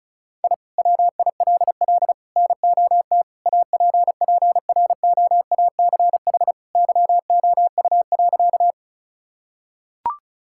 Morse code, heard as a keyed tone, about 36 wpm, 700 Hz. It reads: I WILL NOT APPROACH YOU.
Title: Sentences from Top 1000 Words - Review Part 6 - 35wpm